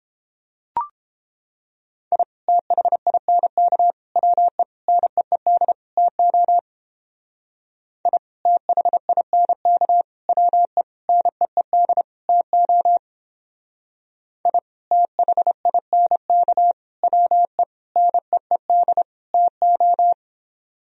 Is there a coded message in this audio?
Yes: Morse code from 2.1 to 20.1 s, heard as a keyed tone, about 31 wpm, 700 Hz. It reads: I THINK WE NEED TO I THINK WE NEED TO I THINK WE NEED TO